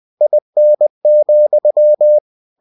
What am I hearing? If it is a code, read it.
Morse code, heard as a keyed tone, about 20 wpm, 600 Hz. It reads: IN,
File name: cw-books / winnie-the-pooh / winnie-20wpm-page-0072